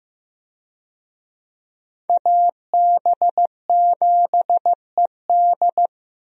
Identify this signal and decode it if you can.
Morse code, heard as a keyed tone, about 15 wpm, 700 Hz. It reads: AB7ED
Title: USA Call Signs - 5 Characters - Repeated 3x - 15wpm